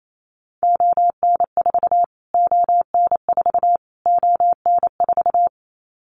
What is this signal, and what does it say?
Morse code, heard as a keyed tone, about 28 wpm, 700 Hz. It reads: ON4 ON4 ON4
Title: Callsign Prefixes for Top 25 DXCC - Repeated 3x - 28wpm